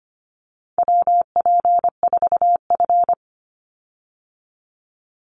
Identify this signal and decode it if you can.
Morse code, heard as a keyed tone, about 25 wpm, 700 Hz. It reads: WP4F
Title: Call Signs Repeated 2x - 25wpm